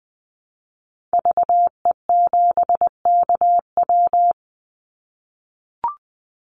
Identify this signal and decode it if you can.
Morse code, heard as a keyed tone, about 20 wpm, 700 Hz. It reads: VE7KW
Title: Call Signs - 20wpm